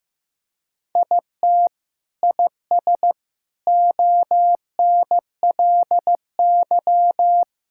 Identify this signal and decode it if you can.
Morse code, heard as a keyed tone, about 16 wpm, 700 Hz. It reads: IT IS ONLY
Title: Sentences from Top 1000 Words - Review Part 5 - 15wpm